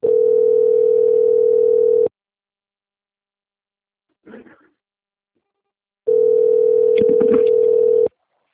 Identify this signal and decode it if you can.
Ringback tone (North America), 440 + 480 Hz.